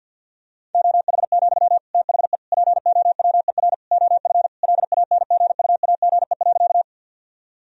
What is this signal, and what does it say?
Morse code, heard as a keyed tone, about 50 wpm, 700 Hz. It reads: OH, THE POWER OF LANGUAGE.
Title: Sentences from Top 1000 Words - Review Part 3 - 50wpm